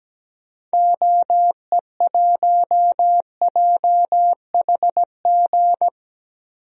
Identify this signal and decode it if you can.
Morse code, heard as a keyed tone, about 17 wpm, 700 Hz. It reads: OE1JHG
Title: DXCC-100 Call Signs - 6 Characters - Repeated 3x - 17wpm